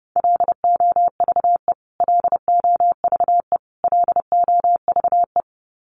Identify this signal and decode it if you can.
Morse code, heard as a keyed tone, about 30 wpm, 700 Hz. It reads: LOVE LOVE LOVE
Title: Top 500 Words Repeated 30wpm